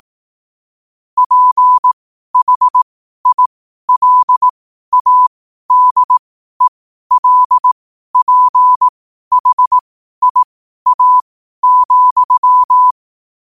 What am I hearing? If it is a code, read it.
Morse code, heard as a keyed tone, about 18 wpm, spaced as for 12 wpm, 1000 Hz. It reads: PHILADELPHIA,
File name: qotdfiles / 12wpm